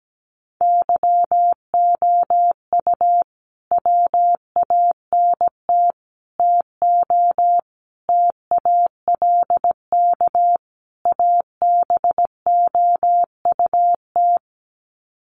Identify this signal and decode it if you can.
Morse code, heard as a keyed tone, about 17 wpm, 700 Hz. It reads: YOU WANT TO TALK ABOUT
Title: Sets of 5 Words Encore 17wpm